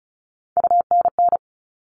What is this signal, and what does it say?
Morse code, heard as a keyed tone, about 34 wpm, 700 Hz. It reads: UNN
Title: All Cut Numbers - 3-Digit 35wpm